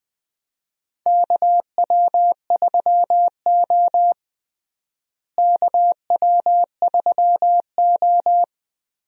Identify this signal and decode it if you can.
Morse code, heard as a keyed tone, about 20 wpm, 700 Hz. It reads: KW3O KW3O
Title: USA Call Signs - 4 Characters - Repeated 3x - 20wpm